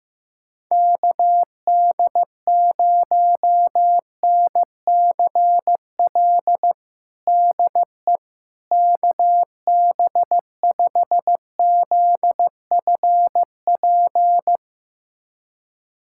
Morse code, heard as a keyed tone, about 15 wpm, 700 Hz. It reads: KD0NCL DE KB5ZFP
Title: USA Callsign DE Callsign - 6 Characters - 15wpm